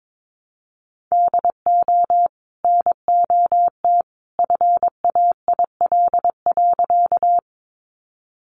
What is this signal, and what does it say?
Morse code, heard as a keyed tone, about 22 wpm, 700 Hz. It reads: DO NOT FAIL.
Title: Sentences from Top 1100 Words 22wpm